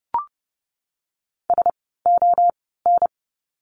Morse code, heard as a keyed tone, about 30 wpm, spaced as for 17 wpm, 700 Hz. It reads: SON
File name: Lesson 17f - Letter u - Words 30wpm